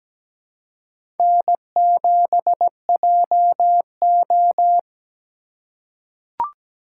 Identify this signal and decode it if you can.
Morse code, heard as a keyed tone, about 17 wpm, 700 Hz. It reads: N7JO
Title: USA Call Signs - 4 Characters - 4x Character Spacing - 17wpm